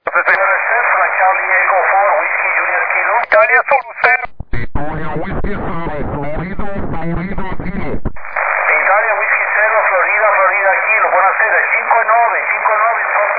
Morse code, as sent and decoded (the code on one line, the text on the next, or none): none